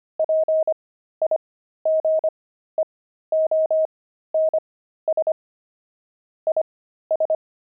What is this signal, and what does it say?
Morse code, heard as a keyed tone, about 25 wpm, spaced as for 13 wpm, 630 Hz. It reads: PIGEONS IS
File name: pangram-fs-13-cs-25-pigeons